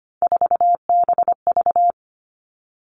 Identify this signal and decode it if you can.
Morse code, heard as a keyed tone, about 25 wpm, 700 Hz. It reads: 4BV
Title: Three Challenging Letter-Number Combinations 25wpm